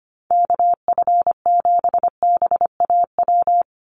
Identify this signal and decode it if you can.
Morse code, heard as a keyed tone, about 25 wpm, 700 Hz. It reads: KF7BAW
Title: USA Call Signs 25wpm